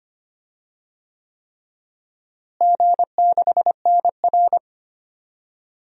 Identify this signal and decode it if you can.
Morse code, heard as a keyed tone, about 25 wpm, 700 Hz. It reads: G6NR
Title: DXCC-100 Call Signs - 4 Characters - Repeated 3x - 25wpm